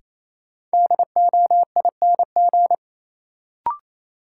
Morse code, heard as a keyed tone, about 28 wpm, 700 Hz. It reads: DOING
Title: Sentences from Top 1400 Words - New Word Drill Alt - 28wpm